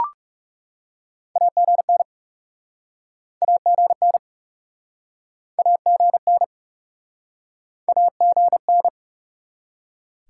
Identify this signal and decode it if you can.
Morse code, heard as a keyed tone, about 44 wpm, 700 Hz. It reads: AGN AGN AGN AGN